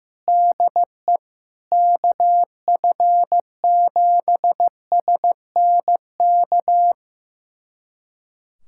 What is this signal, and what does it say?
Morse code, heard as a keyed tone, about 15 wpm, 700 Hz. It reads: DE KF7SNK
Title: USA Callsign DE Callsign 15wpm